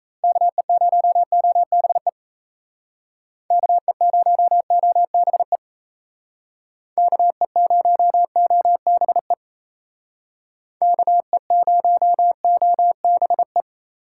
Morse code, heard as a keyed tone, about 37 wpm, 700 Hz. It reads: KE0OBE KE0OBE KE0OBE KE0OBE